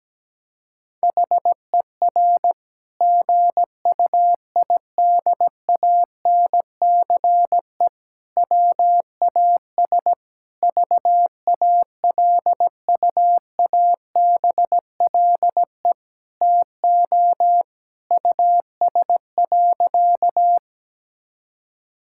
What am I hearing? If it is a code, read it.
Morse code, heard as a keyed tone, about 17 wpm, 700 Hz. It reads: HER GUIDANCE WAS VALUABLE TO US.
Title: Sentences from Top 1600 Words 17wpm